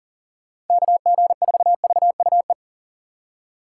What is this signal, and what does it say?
Morse code, heard as a keyed tone, about 52 wpm, 700 Hz. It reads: KG4VUE